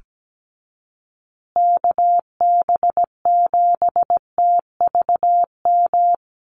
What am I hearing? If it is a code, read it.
Morse code, heard as a keyed tone, about 17 wpm, 700 Hz. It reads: KB7TVM
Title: USA Call Signs Repeated 3x 17wpm